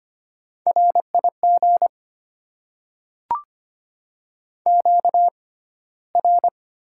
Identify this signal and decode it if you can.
Morse code, heard as a keyed tone, about 25 wpm, spaced as for 15 wpm, 700 Hz. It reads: RIG QR
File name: Top 20 QSO Elements - 6x Character Spacing - 25wpm